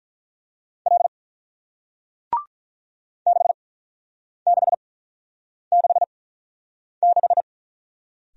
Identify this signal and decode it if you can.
Morse code, heard as a keyed tone, about 48 wpm, spaced as for 9 wpm, 700 Hz. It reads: L 6666